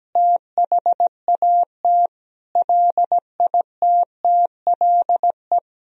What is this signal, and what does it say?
Morse code, heard as a keyed tone, about 17 wpm, 700 Hz. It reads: THAT LITTLE
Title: Sentences from Top 1000 Words - Review Part 1 - 17wpm